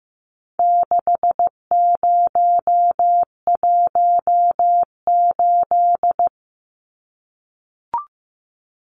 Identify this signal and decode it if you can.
Morse code, heard as a keyed tone, about 15 wpm, 700 Hz. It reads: 6018
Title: Numbers - 4-Digit 15wpm